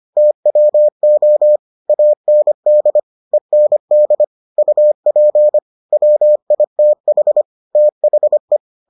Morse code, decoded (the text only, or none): TWO AND END UP WITH THE